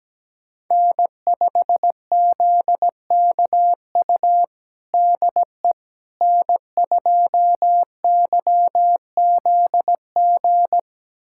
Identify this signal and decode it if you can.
Morse code, heard as a keyed tone, about 17 wpm, 700 Hz. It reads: N5ZKU DE N2YZG
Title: USA Callsign DE Callsign - 5 Characters - 17wpm